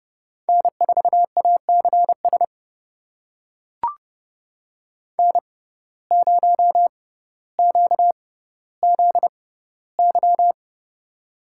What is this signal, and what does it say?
Morse code, heard as a keyed tone, about 30 wpm, spaced as for 15 wpm, 700 Hz. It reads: N4ACS N0QZY
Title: USA Call Signs - 5 Characters - 6x Character Spacing - 30wpm